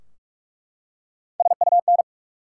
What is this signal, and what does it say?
Morse code, heard as a keyed tone, about 44 wpm, 700 Hz. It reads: SUN
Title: Three Letter Words - 45wpm